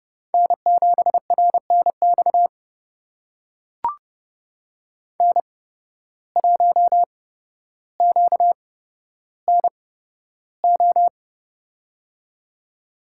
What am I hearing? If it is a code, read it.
Morse code, heard as a keyed tone, about 30 wpm, spaced as for 13 wpm, 700 Hz. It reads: N7RNX N1QNO
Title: USA Call Signs - 5 Characters - 8x Character Spacing - 30wpm